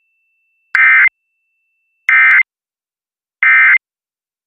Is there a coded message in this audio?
Yes: Emergency Alert System (SAME) tones from 0.7 to 3.8 s, heard as harsh buzzing bursts.